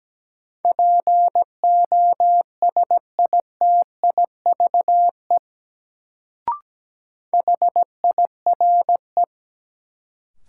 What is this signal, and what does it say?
Morse code, heard as a keyed tone, about 17 wpm, 700 Hz. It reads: POSITIVE HIRE
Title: Sentences from Top 1400 Words - New Word Drill Alt - 17wpm